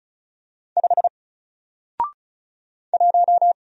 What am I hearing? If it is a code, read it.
Morse code, heard as a keyed tone, about 34 wpm, 700 Hz. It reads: 5 1
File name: Single-Letter-Number-35wpm